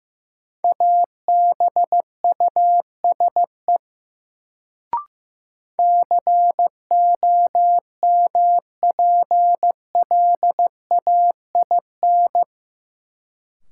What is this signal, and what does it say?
Morse code, heard as a keyed tone, about 15 wpm, 700 Hz. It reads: ABUSE COMPLAIN
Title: Sentences from Top 1300 Words - New Word Drill Alt - 15wpm